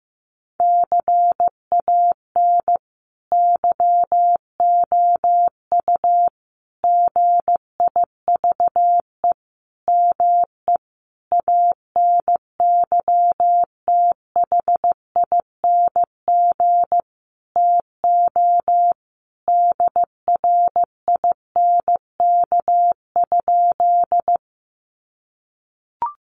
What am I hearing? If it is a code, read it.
Morse code, heard as a keyed tone, about 15 wpm, 700 Hz. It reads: CAN YOU GIVE ME ANYTHING TO DRINK?
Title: Sentences from Top 1500 Words 15wpm